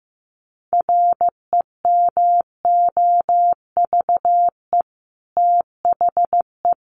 Morse code, heard as a keyed tone, about 15 wpm, 700 Hz. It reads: REMOVE THE
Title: Sentences from Top 1300 Words 15wpm